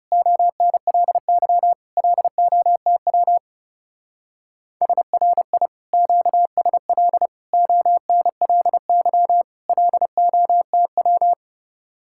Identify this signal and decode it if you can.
Morse code, heard as a keyed tone, about 36 wpm, 700 Hz. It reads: ONLY LOTW SRI QSL ONLY LOTW
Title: Speed-Racing Common QSO Phrases - 30wpm